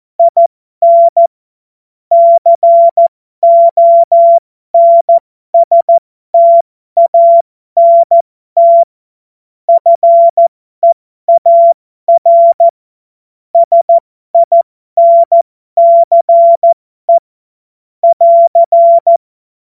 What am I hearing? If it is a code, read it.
Morse code, heard as a keyed tone, about 14 wpm, spaced as for 12 wpm, 680 Hz. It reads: IN CONSTANT FEAR SINCE <AR>
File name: Morsekurs / Witze